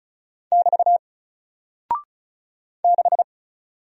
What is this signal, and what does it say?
Morse code, heard as a keyed tone, about 35 wpm, 700 Hz. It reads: <BT> 6
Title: Single Character (Letter:Number:Punctuation:Prosign) Alt 35wpm